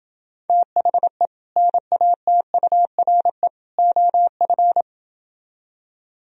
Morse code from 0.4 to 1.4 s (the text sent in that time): THE